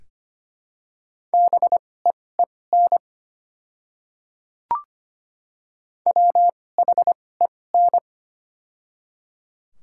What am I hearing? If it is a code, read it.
Morse code, heard as a keyed tone, about 25 wpm, spaced as for 18 wpm, 700 Hz. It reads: BEEN WHEN
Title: 2x Character Spacing - Top 100 Words - 25wpm